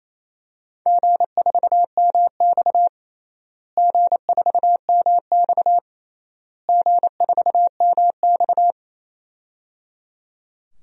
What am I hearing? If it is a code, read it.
Morse code, heard as a keyed tone, about 28 wpm, 700 Hz. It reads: G4MX G4MX G4MX
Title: DXCC-100 Call Signs - 4 Characters - Repeated 3x - 28wpm